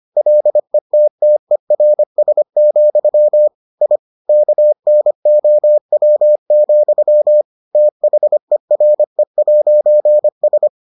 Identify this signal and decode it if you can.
Morse code, heard as a keyed tone, about 25 wpm, 600 Hz. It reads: LETTERS, I KNOW, THERE'S